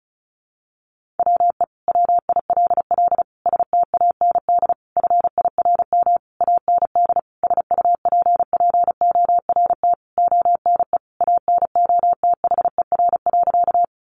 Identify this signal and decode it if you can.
Morse code, heard as a keyed tone, about 34 wpm, 700 Hz. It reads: WE WILL STAND FIRM AND SUPPORT ONE ANOTHER.
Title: Sentences from Top 1400 Words 35wpm